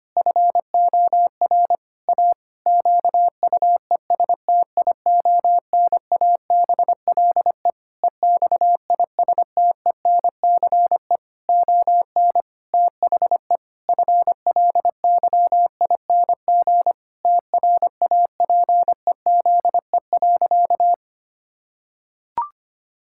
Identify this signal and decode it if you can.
Morse code, heard as a keyed tone, about 25 wpm, 700 Hz. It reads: FOR A QUESTIONABLE EXISTENCE ON THE FLYING TRAPEZE.
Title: Pangram Challenge v2 - 25wpm